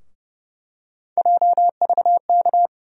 Morse code, heard as a keyed tone, about 32 wpm, 700 Hz. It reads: JVK
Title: USA Call Sign Postfixes - 3 Letters 30wpm